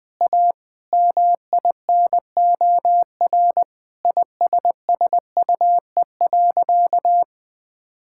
Morse code, heard as a keyed tone, about 20 wpm, 700 Hz. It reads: A MINOR ISSUE.
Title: Sentences with 6-Word Phrases - Spoken Beginning - 20wpm